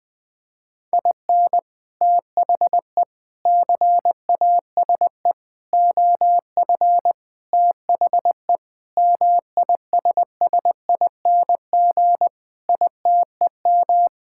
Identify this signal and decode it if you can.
Morse code, heard as a keyed tone, about 20 wpm, 700 Hz. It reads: IN THE CASE OF THE MISSING ITEM
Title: Sentences with 5-Word Phrases - Spoken Beginning - 20wpm